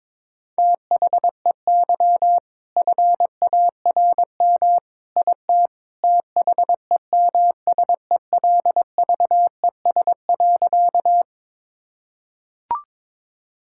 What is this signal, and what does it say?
Morse code, heard as a keyed tone, about 22 wpm, 700 Hz. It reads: THEY FARM IT THEMSELVES.
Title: Sentences from Top 1000 Words - Review Part 5 - 22wpm